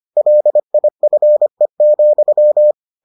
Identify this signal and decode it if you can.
Morse code, heard as a keyed tone, about 25 wpm, 600 Hz. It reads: LIFE,